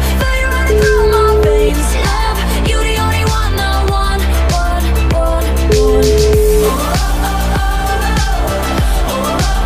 Ringback tone (Europe, 425 Hz), with noise.